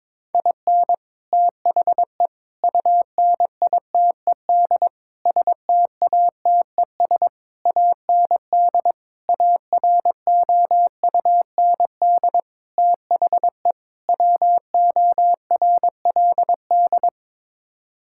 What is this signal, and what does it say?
Morse code, heard as a keyed tone, about 22 wpm, 700 Hz. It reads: IN THE UNITED STATES AND AROUND THE WORLD
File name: practice / 22wpm / Sets of 8 Words - 22wpm